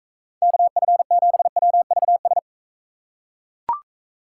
Morse code, heard as a keyed tone, about 42 wpm, 700 Hz. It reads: KF7WVS